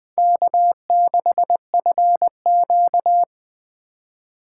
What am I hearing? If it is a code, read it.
Morse code, heard as a keyed tone, about 20 wpm, 700 Hz. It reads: K6FQ
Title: Speed Racing - USA 1x2 Call Signs 17wpm